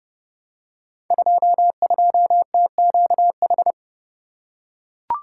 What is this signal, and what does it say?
Morse code, heard as a keyed tone, about 30 wpm, 700 Hz. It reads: 22TQH
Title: Code Groups - 5 Letter:Numbers - 30wpm